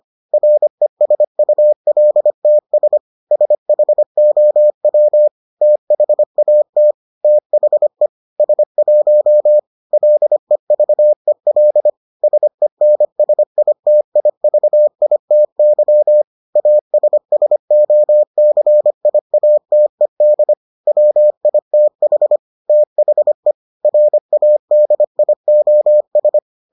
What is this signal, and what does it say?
Morse code, heard as a keyed tone, about 25 wpm, 600 Hz. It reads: RESULTS SHOW THAT THE S1 LEVEL SENSITIVITY ASSOCIATED WITH THE RADIOS